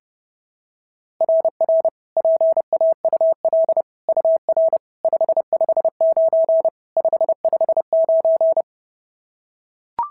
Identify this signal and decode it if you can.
Morse code, heard as a keyed tone, about 30 wpm, 690 Hz. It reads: RR PAUL UR 559 559
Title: SOTA QSO Practice 30wpm